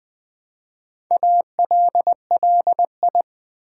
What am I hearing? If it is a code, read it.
Morse code, heard as a keyed tone, about 20 wpm, 700 Hz. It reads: ALLI
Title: Build A Word - 2 Letters At A Time - 8-Letter Words - 20wpm